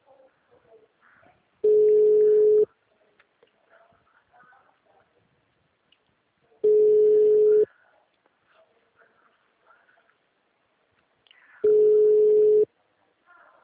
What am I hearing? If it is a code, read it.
Ringback tone (Europe, 425 Hz).